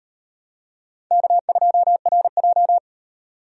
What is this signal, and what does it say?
Morse code, heard as a keyed tone, about 33 wpm, 700 Hz. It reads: K2RJ